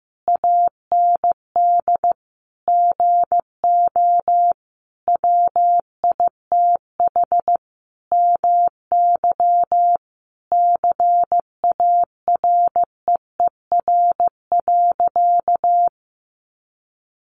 Morse code, heard as a keyed tone, about 15 wpm, 700 Hz. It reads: AND GO WITH MY CAREER.
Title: Sentences from Top 1200 Words 15wpm